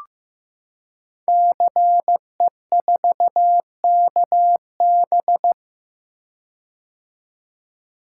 Morse code, heard as a keyed tone, about 15 wpm, 700 Hz. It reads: CE4KB
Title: Code Groups - 5 Letter:Numbers - 15wpm